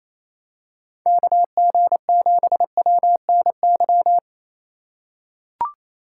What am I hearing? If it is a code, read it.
Morse code, heard as a keyed tone, about 28 wpm, 700 Hz. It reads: KG7WNY
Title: USA Call Signs - 6 Characters - 4x Character Spacing - 28wpm